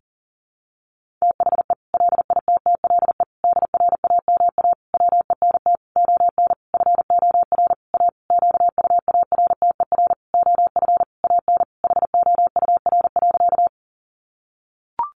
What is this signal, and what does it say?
Morse code, heard as a keyed tone, about 40 wpm, 700 Hz. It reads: THE LITTLE DRAMA WENT ON FOR A QUARTER OF AN HOUR.
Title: Sentences from Top 1500 Words 40wpm